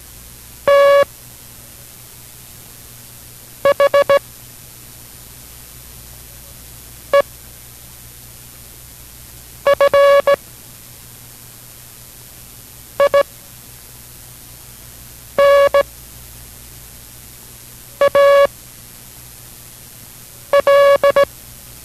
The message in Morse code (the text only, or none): THE FINAL